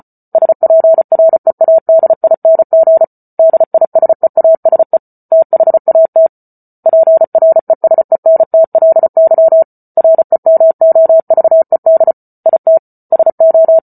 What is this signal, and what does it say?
Morse code, heard as a keyed tone, about 34 wpm, 650 Hz. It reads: SPREADING DISEASE THAT PRESENTLY REMOVED IT SO